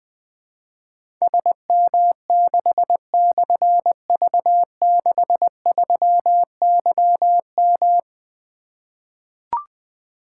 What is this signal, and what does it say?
Morse code, heard as a keyed tone, about 20 wpm, 700 Hz. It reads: SM6/V63YM
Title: Difficult Call Signs 20wpm